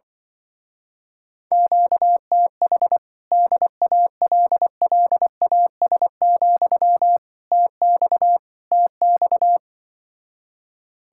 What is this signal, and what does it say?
Morse code, heard as a keyed tone, about 24 wpm, 700 Hz. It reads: QTH DALLAS, TX TX
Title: Speed-Racing Common QSO Phrases - 20wpm